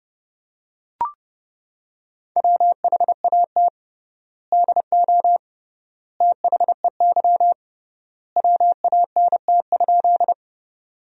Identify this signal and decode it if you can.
Morse code, heard as a keyed tone, about 32 wpm, 700 Hz. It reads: WHAT DO THEY WANT?
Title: 3x Word-Spacing - Sentences from Top 200 Words - 30wpm